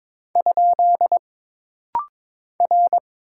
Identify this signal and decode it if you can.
Morse code, heard as a keyed tone, about 22 wpm, 700 Hz. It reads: ? R
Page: This rapid-fire timing is geared to push your ICR (Instant Character Recognition) skills.